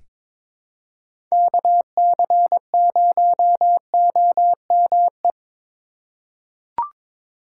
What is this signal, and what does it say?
Morse code, heard as a keyed tone, about 22 wpm, 700 Hz. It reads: KC0OME
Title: USA Call Signs - 6 Characters - 4x Character Spacing - 22wpm